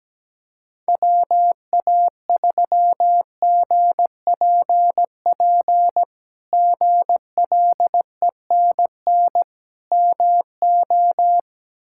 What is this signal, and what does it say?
Morse code, heard as a keyed tone, about 17 wpm, 700 Hz. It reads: WA3GPP GLENN MO
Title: K1USN SST QSO Practice - 17wpm